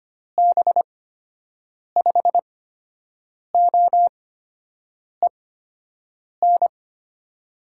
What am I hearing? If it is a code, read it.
Morse code, heard as a keyed tone, about 25 wpm, spaced as for 7 wpm, 700 Hz. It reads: B5OEN
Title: USA Call Signs - 8x Character Spacing - 25wpm